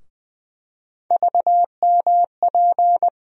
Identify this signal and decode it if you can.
Morse code, heard as a keyed tone, about 20 wpm, 700 Hz. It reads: VMP
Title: USA Call Sign Postfixes - 3 Letters 20wpm